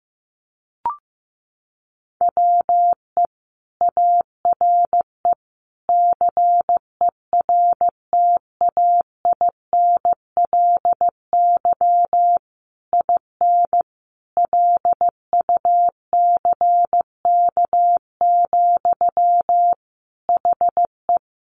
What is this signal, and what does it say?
Morse code, heard as a keyed tone, about 15 wpm, 700 Hz. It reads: WE ARE CERTAINLY IN LUCK, HE
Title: Sentences from Top 1500 Words 15wpm